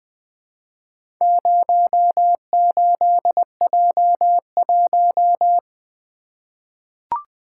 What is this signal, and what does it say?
Morse code, heard as a keyed tone, about 20 wpm, 700 Hz. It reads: 08J1
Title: Four Letter-Number Combinations - 20wpm